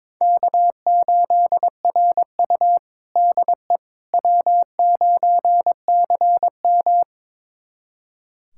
Morse code, heard as a keyed tone, about 22 wpm, 700 Hz. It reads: K8RU DE W9CM
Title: USA Callsign DE Callsign - 4 Characters - 22wpm